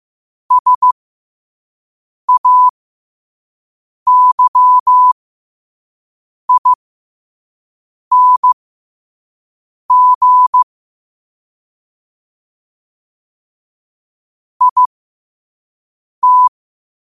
Morse code, heard as a keyed tone, about 15 wpm, spaced as for 5 wpm, 1000 Hz. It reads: SAYING IT
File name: qotdfiles / 5wpm